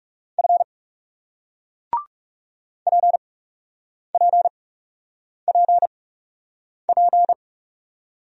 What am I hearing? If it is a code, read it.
Morse code, heard as a keyed tone, about 40 wpm, spaced as for 8 wpm, 700 Hz. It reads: F PPPP